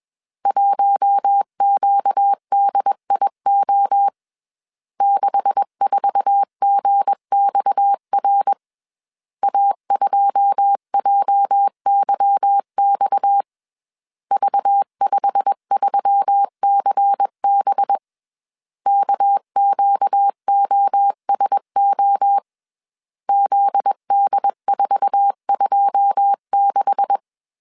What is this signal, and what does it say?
Morse code, heard as a keyed tone, about 21 wpm, 800 Hz. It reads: 1QDIO 64GXR A2JYX V53CB KQOSO ZD426